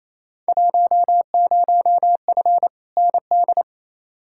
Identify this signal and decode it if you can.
Morse code, heard as a keyed tone, about 28 wpm, 700 Hz. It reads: 10F ND
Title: Field Day Exchanges Encore - 28wpm